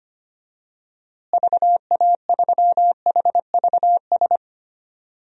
Morse code, heard as a keyed tone, about 25 wpm, 700 Hz. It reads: VA3HVS
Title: DXCC-100 Call Signs - 6 Characters - Repeated 2x - 25wpm